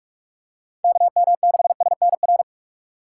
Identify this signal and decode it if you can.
Morse code, heard as a keyed tone, about 41 wpm, 700 Hz. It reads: KM6SNR